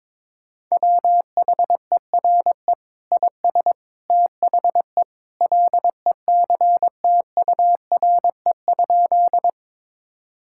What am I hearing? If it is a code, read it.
Morse code, heard as a keyed tone, about 22 wpm, 700 Hz. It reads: WHERE IS THE LECTURE?
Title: Sentences from Top 1600 Words 22wpm